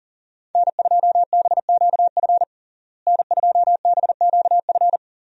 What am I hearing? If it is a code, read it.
Morse code, heard as a keyed tone, about 37 wpm, 700 Hz. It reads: N2BQF N2BQF
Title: USA Call Signs - 5 Characters - Repeated 3x - 40wpm